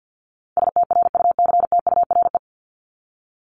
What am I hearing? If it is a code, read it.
Morse code, heard as a keyed tone, about 50 wpm, 700 Hz. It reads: STRUCTURE
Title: Top 751-1000 Words 50wpm